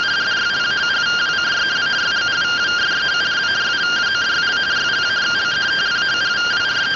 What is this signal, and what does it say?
Two-tone teleprinter signal, about 170 Hz shift.
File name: findings / CT 205 ax mips, iDose 3 0004.gen-05 rtty rx